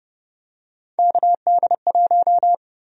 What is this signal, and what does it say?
Morse code, heard as a keyed tone, about 30 wpm, 700 Hz. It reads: KD1
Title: USA Call Sign Prefixes - 30wpm